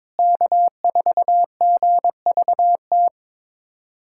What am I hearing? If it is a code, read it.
Morse code, heard as a keyed tone, about 22 wpm, 700 Hz. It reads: K4GVT
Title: USA Callsign DE Callsign - 5 Characters - 22wpm